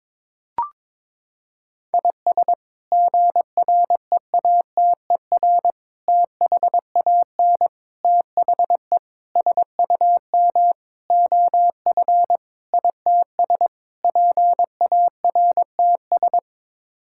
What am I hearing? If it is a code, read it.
Morse code, heard as a keyed tone, about 22 wpm, 700 Hz. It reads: IS GREATER THAN THE SUM OF ITS PARTS
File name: practice / 22wpm / Sets of 8 Words - 22wpm